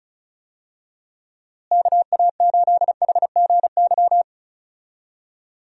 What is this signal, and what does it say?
Morse code, heard as a keyed tone, about 36 wpm, 700 Hz. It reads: KA8HGY
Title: USA Call Signs - 6 Characters - Repeated 2x - 35wpm